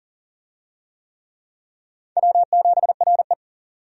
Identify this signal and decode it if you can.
Morse code, heard as a keyed tone, about 40 wpm, 700 Hz. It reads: W7RE
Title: USA Call Signs - 4 Characters - 40wpm